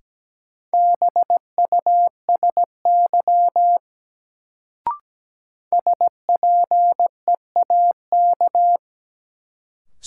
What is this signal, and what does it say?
Morse code, heard as a keyed tone, about 17 wpm, 700 Hz. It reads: BUSY SPEAK
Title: Sentences from Top 1000 Words - Review Part 2 - New Word Drill - 17wpm